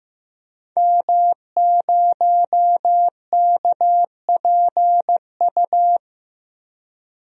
Morse code, heard as a keyed tone, about 15 wpm, 700 Hz. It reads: M0KPU
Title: DXCC-100 Call Signs - 5 Characters - Repeated 3x - 15wpm